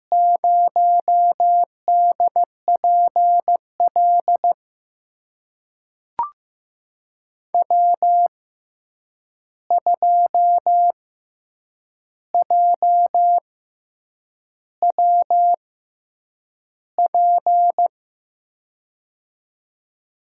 Morse code, heard as a keyed tone, about 16 wpm, spaced as for 7 wpm, 700 Hz. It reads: 0DPL W2JWP
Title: USA Call Signs - 5 Characters - 6x Character Spacing - 15wpm